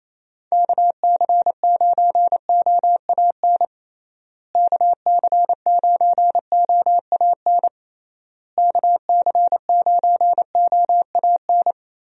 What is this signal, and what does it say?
Morse code, heard as a keyed tone, about 28 wpm, 700 Hz. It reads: KC9OAN KC9OAN KC9OAN
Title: USA Call Signs - 6 Characters - Repeated 3x - 28wpm